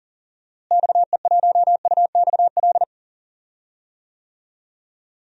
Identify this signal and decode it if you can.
Morse code, heard as a keyed tone, about 39 wpm, 700 Hz. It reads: XE1UXL
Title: DXCC-100 Call Signs - 6 Characters - Repeated 2x - 40wpm